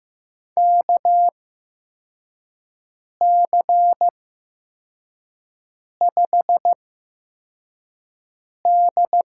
Morse code, heard as a keyed tone, about 15 wpm, spaced as for 4 wpm, 700 Hz. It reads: KC5D
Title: USA Call Signs - 8x Character Spacing - 15wpm